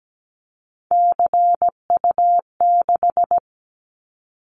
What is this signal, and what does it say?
Morse code, heard as a keyed tone, about 17 wpm, 700 Hz. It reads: CU6
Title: Callsign Prefixes for Top 100 DXCC - 17wpm